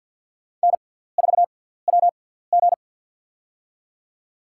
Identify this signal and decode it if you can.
Morse code, heard as a keyed tone, about 48 wpm, spaced as for 17 wpm, 690 Hz. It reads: N4WG
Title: USA Call Signs - 4 Characters - 6x Character Spacing - 50wpm